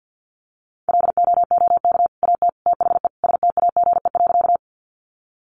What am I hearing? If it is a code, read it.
Morse code, heard as a keyed tone, about 49 wpm, 700 Hz. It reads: LOOK AT THE STAGE.
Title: Sentences from Top 1200 Words 50wpm